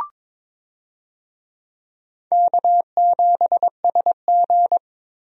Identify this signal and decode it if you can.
Morse code, heard as a keyed tone, about 22 wpm, 700 Hz. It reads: K7SG
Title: USA Call Signs - 4 Characters - Repeated 3x - 22wpm